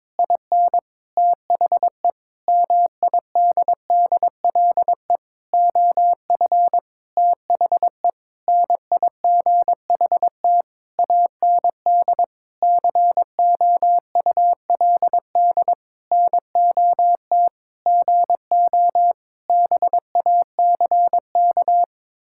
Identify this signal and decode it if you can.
Morse code, heard as a keyed tone, about 22 wpm, 700 Hz. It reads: IN THE MIDDLE OF THE NIGHT AND COULD NOT GO BACK
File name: Sentences with 7-Word Phrases - 22wpm